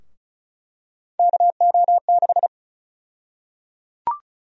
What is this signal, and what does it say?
Morse code, heard as a keyed tone, about 34 wpm, 700 Hz. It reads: KO6
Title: USA Call Sign Prefixes - 35wpm